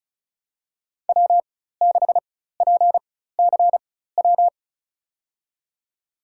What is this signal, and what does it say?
Morse code, heard as a keyed tone, about 36 wpm, spaced as for 16 wpm, 700 Hz. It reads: W6PCW
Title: USA Call Signs - 5 Characters - 4x Character Spacing - 35wpm